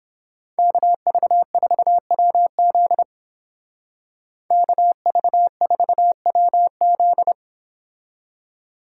Morse code, heard as a keyed tone, about 29 wpm, 700 Hz. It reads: KV4WZ KV4WZ